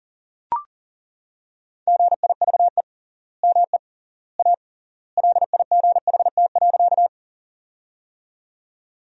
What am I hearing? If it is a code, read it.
Morse code, heard as a keyed tone, about 40 wpm, 700 Hz. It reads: GIVE ME A LIGHT.
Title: 3x Word-Spacing - Sentences from Top 200 Words - 40wpm